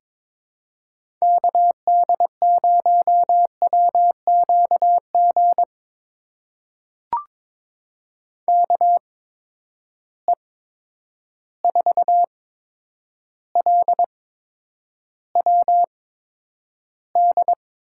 Morse code, heard as a keyed tone, about 23 wpm, spaced as for 10 wpm, 700 Hz. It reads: KD0WQG KE4LWD
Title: USA Call Signs - 6 Characters - 8x Character Spacing - 22wpm